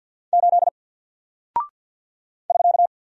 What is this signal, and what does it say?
Morse code, heard as a keyed tone, about 48 wpm, 700 Hz. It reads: 8 <SK>
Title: Single Character (Letter:Number:Punctuation:Prosign) Alt 50wpm